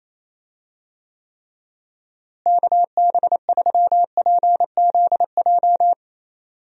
Morse code, heard as a keyed tone, about 27 wpm, 700 Hz. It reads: KB3PZJ
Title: USA Call Signs - 6 Characters - Repeated 2x - 28wpm